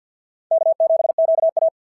Morse code, heard as a keyed tone, about 52 wpm, 690 Hz. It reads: K7QA
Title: SOTA QSO Practice 50wpm